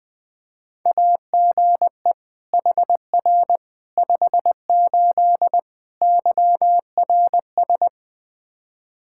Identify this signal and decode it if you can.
Morse code, heard as a keyed tone, about 21 wpm, 700 Hz. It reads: AGE HR 58 YRS
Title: Practice QSO - Exchange 3 - More Info - 20wpm